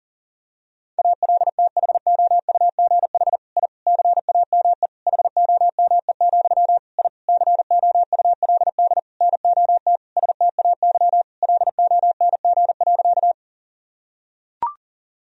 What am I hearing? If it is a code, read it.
Morse code, heard as a keyed tone, about 40 wpm, 700 Hz. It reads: ALTHOUGH I CAME HOME, I COULD NOT STAY LONG.
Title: Sentences from Top 1000 Words - Review Part 3 - 40wpm